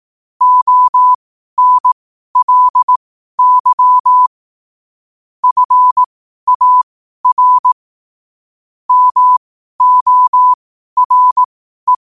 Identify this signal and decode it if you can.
Morse code, heard as a keyed tone, about 18 wpm, spaced as for 12 wpm, 1000 Hz. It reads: ONLY FAR MORE
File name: qotdfiles / 12wpm